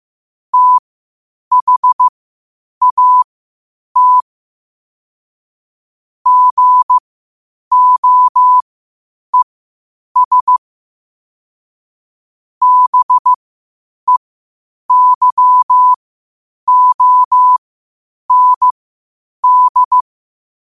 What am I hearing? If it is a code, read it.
Morse code, heard as a keyed tone, about 15 wpm, spaced as for 8 wpm, 1000 Hz. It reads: THAT GOES BEYOND
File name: qotdfiles / 8wpm